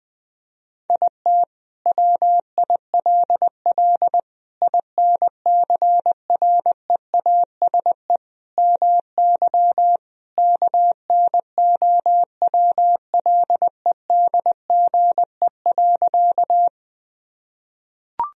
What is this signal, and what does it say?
Morse code, heard as a keyed tone, about 20 wpm, 700 Hz. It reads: IT WILL INCREASE MY KNOWLEDGE.
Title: Sentences from Top 1000 Words - Review Part 4 - 20wpm